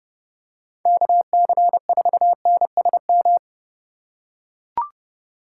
Morse code, heard as a keyed tone, about 32 wpm, 700 Hz. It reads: KC4NSM
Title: USA Call Signs - 6 Characters - 4x Character Spacing - 30wpm